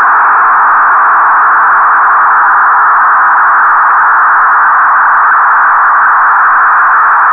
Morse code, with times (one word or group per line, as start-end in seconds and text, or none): none